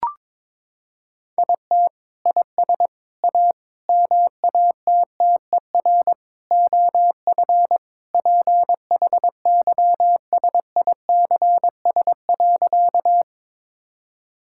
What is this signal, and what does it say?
Morse code, heard as a keyed tone, about 22 wpm, 700 Hz. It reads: IT IS A MATTER OF PHYSICS.
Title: Sentences from Top 1000 Words - Review Part 5 - 22wpm